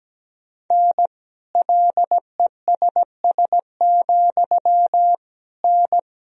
Morse code, heard as a keyed tone, about 17 wpm, 700 Hz. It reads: N LESS, N
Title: Sentences from Top 400 Words and Abbreviations 17wpm